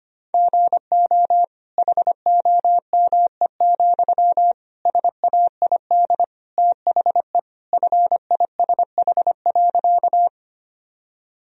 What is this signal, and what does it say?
Morse code, heard as a keyed tone, about 25 wpm, 700 Hz. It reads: GO HOME, SAID THE FISH.
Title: Sentences from Top 1000 Words - Review Part 1 - 25wpm